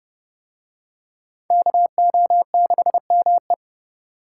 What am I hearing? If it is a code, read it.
Morse code, heard as a keyed tone, about 29 wpm, 700 Hz. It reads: KO6ME